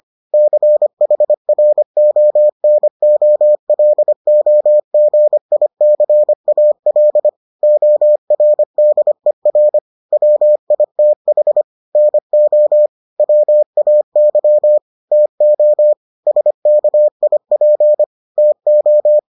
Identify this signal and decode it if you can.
Morse code, heard as a keyed tone, about 25 wpm, 600 Hz. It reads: CHRONOLOGICAL ORDER WITH NO WAY TO SKIP TO